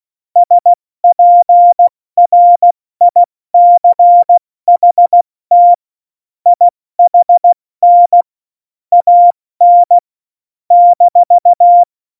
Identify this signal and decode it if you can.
Morse code, heard as a keyed tone, about 16 wpm, 700 Hz. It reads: SPRICHT IHN AN -
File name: Morsekurs / Witze / Witz 27. Oktober 2020, 70 F80